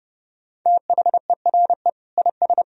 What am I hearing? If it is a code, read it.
Morse code, heard as a keyed tone, about 30 wpm, 700 Hz. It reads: THERE IS
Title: Sentences from Top 1000 Words - Review Part 4 - 30wpm